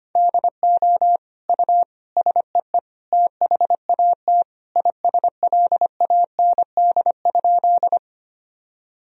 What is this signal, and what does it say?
Morse code, heard as a keyed tone, about 25 wpm, 700 Hz. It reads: DO U SEE THAT ISLAND?
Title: Sentences from Top 500 Words and Abbreviations 25wpm